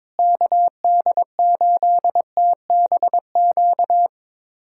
Morse code, heard as a keyed tone, about 21 wpm, 700 Hz. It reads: KD8TBQ
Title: USA Callsign DE Callsign 22wpm